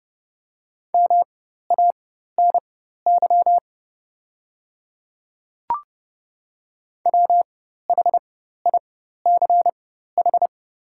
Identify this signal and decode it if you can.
Morse code, heard as a keyed tone, about 30 wpm, spaced as for 14 wpm, 700 Hz. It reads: MANY WHICH
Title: 4x Character Spacing - Top 100 Words - 30wpm